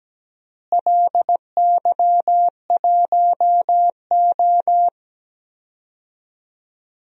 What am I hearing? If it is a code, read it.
Morse code, heard as a keyed tone, about 17 wpm, 700 Hz. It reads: LY1O
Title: DXCC-100 Call Signs - 4 Characters - Repeated 3x - 17wpm